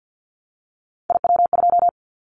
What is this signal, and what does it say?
Morse code, heard as a keyed tone, about 50 wpm, 700 Hz. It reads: IW2
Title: Callsign Prefixes for Top 25 DXCC - Repeated 3x - 50wpm